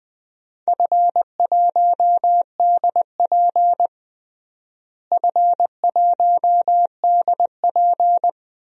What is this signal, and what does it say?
Morse code, heard as a keyed tone, about 21 wpm, 700 Hz. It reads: F1DP F1DP
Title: DXCC-100 Call Signs - 4 Characters - Repeated 3x - 20wpm